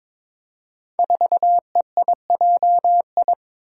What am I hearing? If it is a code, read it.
Morse code, heard as a keyed tone, about 22 wpm, 700 Hz. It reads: 4EIJI
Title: Five Challenging Letter-Number Combinations 22wpm